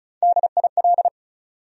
Morse code, heard as a keyed tone, about 34 wpm, 700 Hz. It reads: DIL
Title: USA Call Sign Postfixes - 3 Letters 35wpm